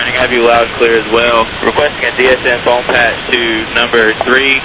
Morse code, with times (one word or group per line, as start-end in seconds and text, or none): none